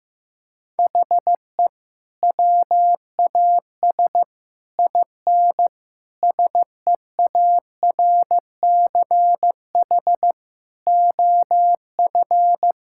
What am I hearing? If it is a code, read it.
Morse code, heard as a keyed tone, about 14 wpm, 700 Hz. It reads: HE WAS IN SEARCH OF
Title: Sentences from Top 1000 Words - Review Part 5 - 15wpm